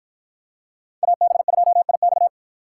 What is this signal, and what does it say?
Morse code, heard as a keyed tone, about 52 wpm, 700 Hz. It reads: AB2IX